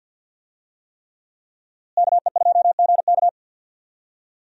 Morse code, heard as a keyed tone, about 46 wpm, 700 Hz. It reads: KE2GK
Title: USA Call Signs - 5 Characters - Repeated 3x - 50wpm